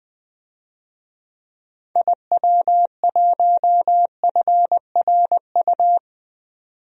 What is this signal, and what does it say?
Morse code, heard as a keyed tone, about 20 wpm, 700 Hz. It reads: IW1FRU
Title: DXCC-100 Call Signs - 6 Characters - 20wpm